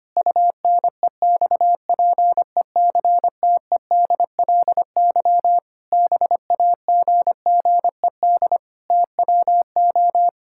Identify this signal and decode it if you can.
Morse code, heard as a keyed tone, about 25 wpm, 700 Hz. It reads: UNEXPECTEDLY BAGGED TWO
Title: Pangram Challenge v2 - 25wpm